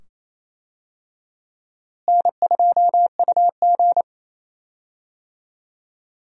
Morse code, heard as a keyed tone, about 28 wpm, 700 Hz. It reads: N2UG
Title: USA Call Signs - 4 Characters - Repeated 2x - 28wpm